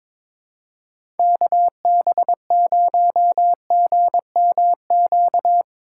Morse code, heard as a keyed tone, about 22 wpm, 700 Hz. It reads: KB0GMQ
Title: USA Call Signs - 6 Characters - 4x Character Spacing - 22wpm